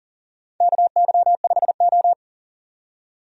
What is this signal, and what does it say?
Morse code, heard as a keyed tone, about 37 wpm, 700 Hz. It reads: KY5O
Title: USA Call Signs - 4 Characters - 40wpm